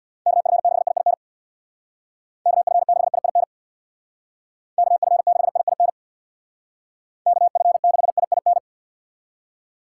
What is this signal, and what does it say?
Morse code, heard as a keyed tone, about 65 wpm, 700 Hz. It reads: KF6IIN KF6IIN KF6IIN KF6IIN